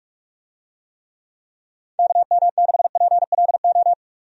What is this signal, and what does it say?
Morse code, heard as a keyed tone, about 46 wpm, 700 Hz. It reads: KM6PLO
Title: USA Call Signs - 6 Characters - 45wpm